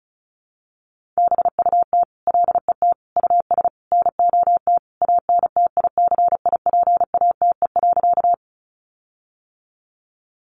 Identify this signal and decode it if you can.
Morse code, heard as a keyed tone, about 34 wpm, 700 Hz. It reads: BUT LET US NOT ANTICIPATE.
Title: Sentences from Top 1500 Words 35wpm